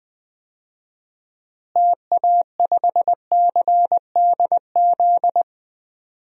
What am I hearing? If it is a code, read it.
Morse code, heard as a keyed tone, about 20 wpm, 700 Hz. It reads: TA5CDZ
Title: DXCC-100 Call Signs - 6 Characters - Repeated 2x - 20wpm